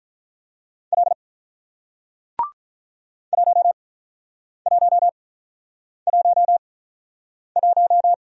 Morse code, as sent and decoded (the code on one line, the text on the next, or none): .-.. / .---- .---- .---- .----
L 1111